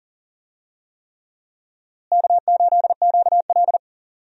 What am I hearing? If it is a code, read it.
Morse code, heard as a keyed tone, about 40 wpm, 700 Hz. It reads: K8QL
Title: USA Call Signs - 4 Characters - Repeated 3x - 40wpm